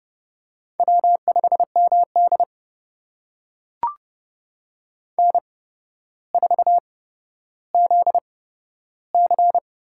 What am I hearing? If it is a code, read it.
Morse code, heard as a keyed tone, about 30 wpm, spaced as for 13 wpm, 700 Hz. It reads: W5MD N4ZC
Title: USA Call Signs - 8x Character Spacing - 30wpm